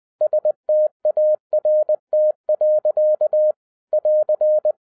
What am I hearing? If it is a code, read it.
Morse code, heard as a keyed tone, about 20 wpm, 610 Hz. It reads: START. <AR>
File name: 2011-10-mp3 / n03-TARO JR0QWW 20wpm